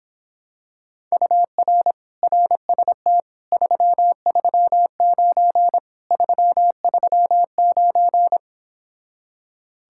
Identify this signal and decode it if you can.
Morse code, heard as a keyed tone, about 26 wpm, 700 Hz. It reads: UR RST 339 339